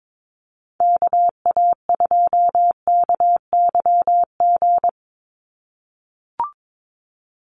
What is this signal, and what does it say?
Morse code, heard as a keyed tone, about 22 wpm, 700 Hz. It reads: KA2KYG
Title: USA Call Signs 22wpm